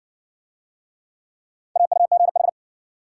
Morse code, heard as a keyed tone, about 53 wpm, 700 Hz. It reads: AUGH